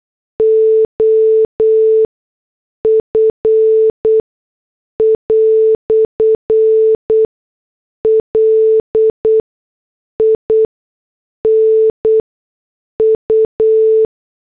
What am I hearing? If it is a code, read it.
Morse code, heard as a keyed tone, about 8 wpm, spaced as for 6 wpm, 440 Hz. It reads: OF "LINU
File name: morse / headlines